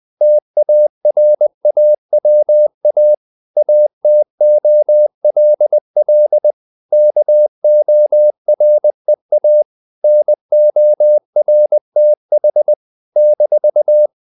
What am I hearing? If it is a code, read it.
Morse code, heard as a keyed tone, about 20 wpm, 600 Hz. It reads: TARAWA ATOLL KOREA NORTH -
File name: CW / Practice / NationalCapitals20